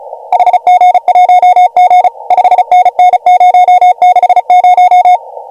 Morse code, heard as a keyed tone, about 35 wpm, 690 Hz. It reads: HG1G 5NN 060